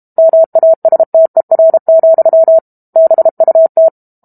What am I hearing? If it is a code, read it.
Morse code, heard as a keyed tone, about 32 wpm, 650 Hz. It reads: MASTER, BUT
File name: Book Two / 06-Chapter 6 - 32 WPM